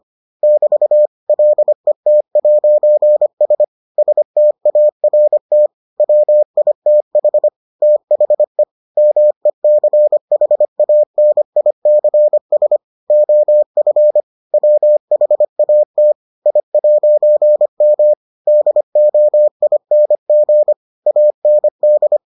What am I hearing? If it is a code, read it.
Morse code, heard as a keyed tone, about 25 wpm, 600 Hz. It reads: <BT> LET'S START WITH THE MECHANICS OF WHAT I'M DOING AND